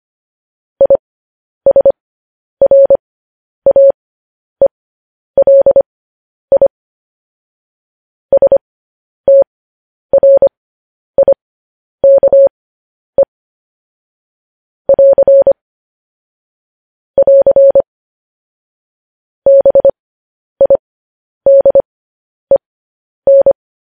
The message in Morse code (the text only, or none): ISRAELI STRIKE <AR> <AR> BIDEN